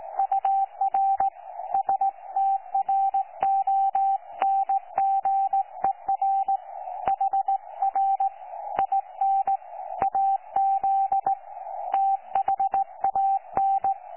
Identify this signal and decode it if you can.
Morse code, heard as a keyed tone, about 17 wpm, 780 Hz. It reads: UR STRONGER HR IN AZ THAN
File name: 12 39Z 18088.2kHz utah aa7fv k9tjl